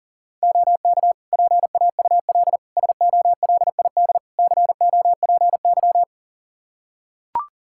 Morse code, heard as a keyed tone, about 40 wpm, 700 Hz. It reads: OK PAUL SOLID COPY
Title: K1USN SST QSO Practice 40wpm